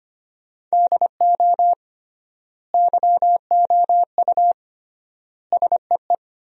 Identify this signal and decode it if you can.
Morse code, heard as a keyed tone, about 25 wpm, 700 Hz. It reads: DO YOU SEE
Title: 3x Word-Spacing - Sentences from Top 100 Words - 25wpm